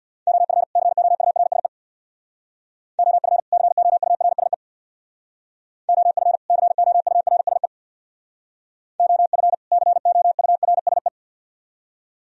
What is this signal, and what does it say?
Morse code, heard as a keyed tone, about 63 wpm, 700 Hz. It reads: OF COURSE OF COURSE OF COURSE OF COURSE